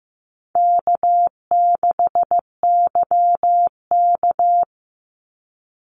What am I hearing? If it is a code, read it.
Morse code, heard as a keyed tone, about 15 wpm, 700 Hz. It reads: K6YK
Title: NA SOTA Chaser Callsigns 15wpm